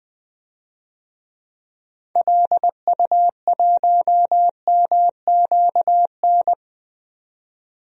Morse code, heard as a keyed tone, about 21 wpm, 700 Hz. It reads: LU1MQN
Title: DXCC-100 Call Signs - 6 Characters - 20wpm